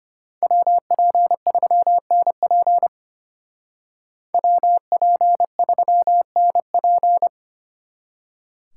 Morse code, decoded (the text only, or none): WP3NP WP3NP